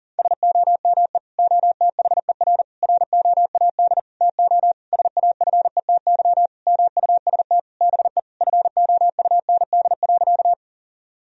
Morse code, heard as a keyed tone, about 37 wpm, 700 Hz. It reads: SOME OTHER ROAD TO SAFETY MUST BE FOUND.
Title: Sentences from Top 1000 Words - Review Part 4 - 40wpm